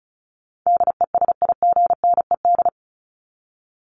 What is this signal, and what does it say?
Morse code, heard as a keyed tone, about 34 wpm, 700 Hz. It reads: DESIGNED
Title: ED Suffix - Top 100 Words with Base Words 35wpm